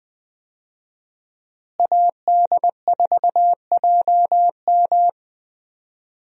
Morse code, heard as a keyed tone, about 20 wpm, 700 Hz. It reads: AD4JM
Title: USA Call Signs - 5 Characters - Repeated 3x - 20wpm